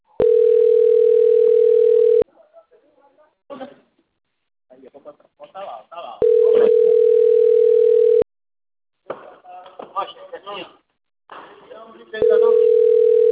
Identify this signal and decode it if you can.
Ringback tone (North America), 440 + 480 Hz, with speech.
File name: rg-1600-unknown- - - .24696